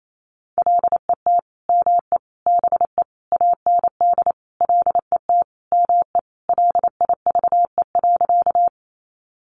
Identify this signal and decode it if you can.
Morse code, heard as a keyed tone, about 28 wpm, 700 Hz. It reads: LET ME BE AND LET ME LIVE.
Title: Sentences from Top 300 Words 28wpm